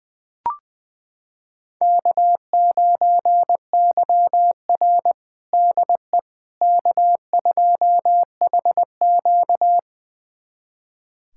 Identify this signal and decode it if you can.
Morse code, heard as a keyed tone, about 20 wpm, 700 Hz. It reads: K9YR DE K2HQ
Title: USA Callsign DE Callsign - 4 Characters - 20wpm